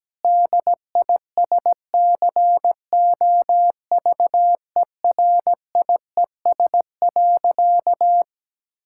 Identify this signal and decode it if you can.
Morse code, heard as a keyed tone, about 17 wpm, 700 Hz. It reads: DISCOVERIES.
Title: Pangram Challenge v2 - 17wpm